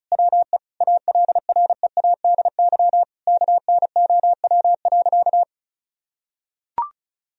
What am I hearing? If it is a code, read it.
Morse code, heard as a keyed tone, about 36 wpm, 700 Hz. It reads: WE ALREADY KNOW.
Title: Sentences from Top 1000 Words - Review Part 1 - 35wpm